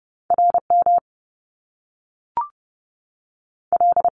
Morse code, heard as a keyed tone, about 30 wpm, 700 Hz. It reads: R M L
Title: Top Two Letter Combinations in Words 30wpm